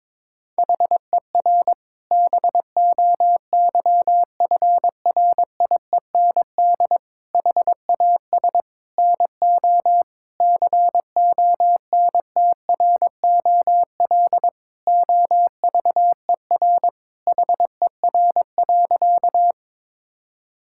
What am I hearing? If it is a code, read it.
Morse code, heard as a keyed tone, about 22 wpm, 700 Hz. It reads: HER BOYFRIEND HAS NO CONTROL OVER HER.
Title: Sentences from Top 1000 Words - Review Part 5 - 22wpm